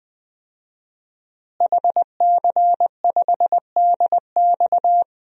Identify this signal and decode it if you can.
Morse code, heard as a keyed tone, about 21 wpm, 700 Hz. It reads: HC5DX
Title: DXCC-100 Call Signs - 5 Characters - Repeated 3x - 20wpm